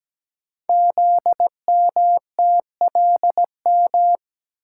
Morse code, heard as a keyed tone, about 17 wpm, 700 Hz. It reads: ZMTLM
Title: Code Groups - 5 Letters - 17wpm